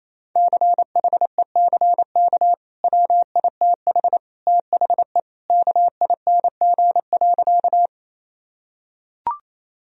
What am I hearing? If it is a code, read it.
Morse code, heard as a keyed tone, about 28 wpm, 700 Hz. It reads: CHECK WITH THE KING.
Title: Sentences from Top 1000 Words - Review Part 3 - 28wpm